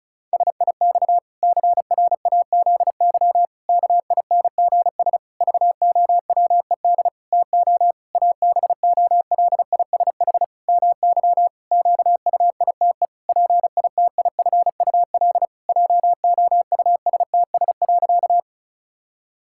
Morse code, heard as a keyed tone, about 34 wpm, 700 Hz. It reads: SIX CRAZY KINGS VOWED TO ABOLISH MY QUITE PITIFUL JOUSTS.
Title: Pangram Challenge v2 - 35wpm